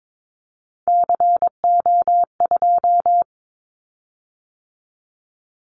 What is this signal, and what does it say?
Morse code, heard as a keyed tone, about 22 wpm, 700 Hz. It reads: CO2
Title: Callsign Prefixes for Top 100 DXCC - 22wpm